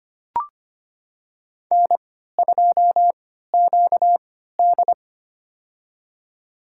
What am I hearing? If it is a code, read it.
Morse code, heard as a keyed tone, about 25 wpm, spaced as for 14 wpm, 700 Hz. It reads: N2QD
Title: USA Call Signs - 3x Character Spacing - 25wpm